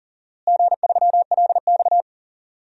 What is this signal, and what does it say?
Morse code, heard as a keyed tone, about 40 wpm, 700 Hz. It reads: G3LX
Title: DXCC-100 Call Signs - 4 Characters - Repeated 3x - 40wpm